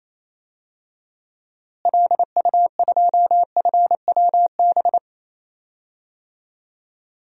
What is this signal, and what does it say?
Morse code, heard as a keyed tone, about 28 wpm, 700 Hz. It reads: LU2FWB
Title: DXCC-100 Call Signs - 6 Characters - Repeated 2x - 28wpm